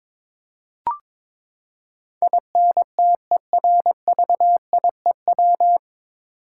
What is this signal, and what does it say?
Morse code, heard as a keyed tone, about 22 wpm, 700 Hz. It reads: INTERVIEW
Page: Morse Code Ninja practice 22wpm